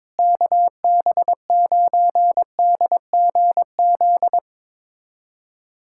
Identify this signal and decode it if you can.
Morse code, heard as a keyed tone, about 22 wpm, 700 Hz. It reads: KB9DGZ
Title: USA Callsign DE Callsign - 6 Characters - 22wpm